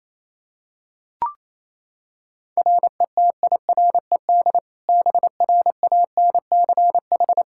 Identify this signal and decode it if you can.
Morse code, heard as a keyed tone, about 28 wpm, 700 Hz. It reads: RETIRED BRANCH
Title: Practice QSO - Exchange 3 - More Info - 28wpm